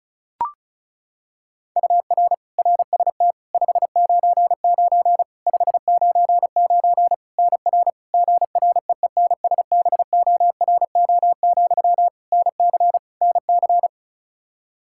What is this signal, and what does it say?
Morse code, heard as a keyed tone, about 36 wpm, 700 Hz. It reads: UR RST 599 599 NR GREENSBORO, NC NC
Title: Practice QSO - Exchange 1 - Beginning - 35wpm